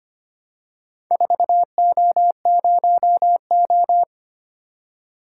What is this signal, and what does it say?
Morse code, heard as a keyed tone, about 25 wpm, 700 Hz. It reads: 4O0O
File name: practice / 25wpm / Four Letter-Number Combinations - 25wpm